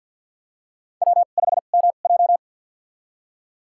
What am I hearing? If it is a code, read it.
Morse code, heard as a keyed tone, about 53 wpm, spaced as for 37 wpm, 700 Hz. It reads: W5MJ
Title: USA Call Signs - 4 Characters - 2x Character Spacing - 50wpm